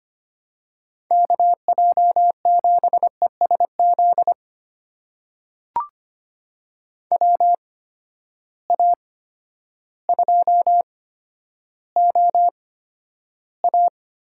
Morse code, heard as a keyed tone, about 25 wpm, spaced as for 11 wpm, 700 Hz. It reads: KJ7ESZ WA2OA
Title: USA Call Signs - 6 Characters - 8x Character Spacing - 25wpm